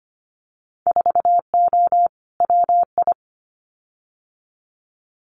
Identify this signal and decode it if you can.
Morse code, heard as a keyed tone, about 25 wpm, 700 Hz. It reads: 4O WI
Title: Winter Field Day Exchanges 25wpm